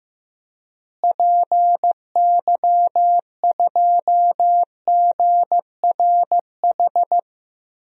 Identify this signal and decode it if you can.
Morse code, heard as a keyed tone, about 16 wpm, 700 Hz. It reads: PY2GRH
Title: DXCC-100 Call Signs - 6 Characters - Repeated 2x - 15wpm